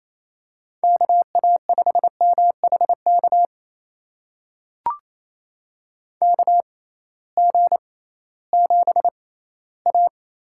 Morse code, heard as a keyed tone, about 28 wpm, spaced as for 16 wpm, 700 Hz. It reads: KA5MHK KG7A
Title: USA Call Signs - 6 Characters - 6x Character Spacing - 28wpm